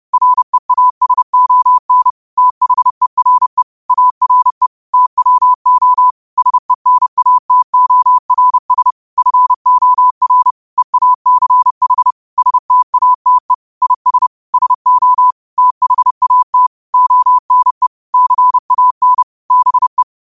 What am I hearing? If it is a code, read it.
Morse code, heard as a keyed tone, about 30 wpm, 1000 Hz. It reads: REASON THERE ARE TWO SENATORS FOR EACH STATE IS SO THAT ONE CAN BE
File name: qotdfiles / 30wpm